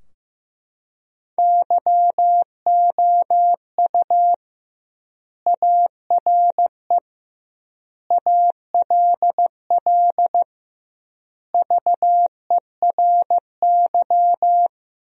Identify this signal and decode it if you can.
Morse code, heard as a keyed tone, about 15 wpm, 700 Hz. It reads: YOU ARE ALL VERY
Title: 2x Word-Spacing - Sentences from Top 200 Words - 15wpm